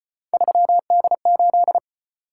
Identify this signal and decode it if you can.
Morse code, heard as a keyed tone, about 33 wpm, 700 Hz. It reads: 3D8